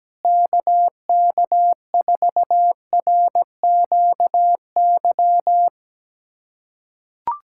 Morse code, heard as a keyed tone, about 17 wpm, 700 Hz. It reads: KK4RQY
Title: USA Callsign DE Callsign 17wpm